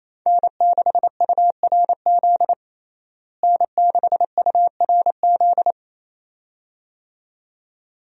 Morse code, heard as a keyed tone, about 28 wpm, 700 Hz. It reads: N6URZ N6URZ
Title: USA Call Signs - 5 Characters - Repeated 2x - 28wpm